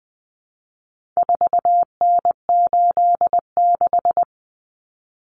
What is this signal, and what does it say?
Morse code, heard as a keyed tone, about 20 wpm, 700 Hz. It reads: 4N86
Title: Common Cut Numbers - 4-Digit 20wpm